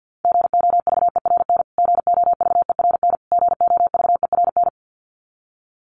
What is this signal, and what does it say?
Morse code, heard as a keyed tone, about 50 wpm, 700 Hz. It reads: GOVERN GOVERN GOVERN
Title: Top 500 Words Repeated 50wpm